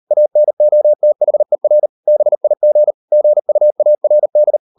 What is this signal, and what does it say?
Morse code, heard as a keyed tone, about 38 wpm, 600 Hz. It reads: ANOTHER BIG GUARD